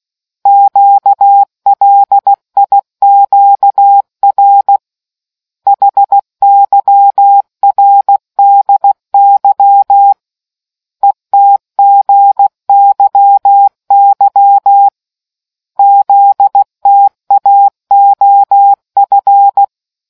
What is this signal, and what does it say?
Morse code, heard as a keyed tone, about 16 wpm, 800 Hz. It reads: QLIQR HYRDY ETGYY ZTAOF